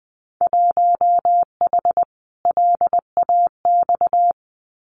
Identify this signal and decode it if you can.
Morse code, heard as a keyed tone, about 20 wpm, 700 Hz. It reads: 1H LAX
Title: Winter Field Day Exchanges 20wpm